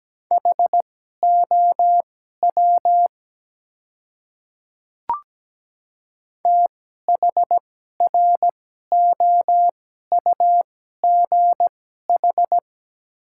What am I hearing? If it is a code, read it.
Morse code, heard as a keyed tone, about 17 wpm, spaced as for 12 wpm, 700 Hz. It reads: HOW THROUGH
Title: 2x Character Spacing - Top 200 Words - 17wpm